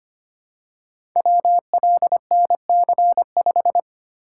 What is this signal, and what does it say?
Morse code, heard as a keyed tone, about 25 wpm, 700 Hz. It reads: WLNC5
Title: Code Groups - 5 Letter:Numbers - 25wpm